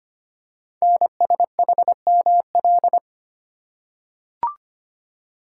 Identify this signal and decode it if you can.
Morse code, heard as a keyed tone, about 25 wpm, 700 Hz. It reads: NSHML
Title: Code Groups - 5 Letters - 25wpm